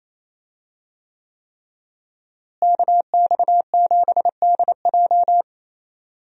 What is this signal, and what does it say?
Morse code, heard as a keyed tone, about 28 wpm, 700 Hz. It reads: KX7DJ
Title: USA Call Signs - 5 Characters - Repeated 3x - 28wpm